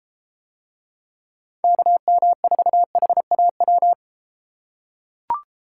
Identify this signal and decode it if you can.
Morse code, heard as a keyed tone, about 31 wpm, 700 Hz. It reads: KM4HAW